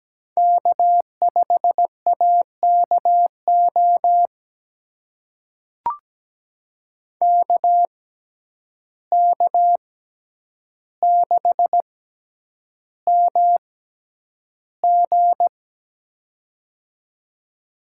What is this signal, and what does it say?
Morse code, heard as a keyed tone, about 17 wpm, spaced as for 9 wpm, 700 Hz. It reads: K5AKO KK6MG
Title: USA Call Signs - 5 Characters - 6x Character Spacing - 17wpm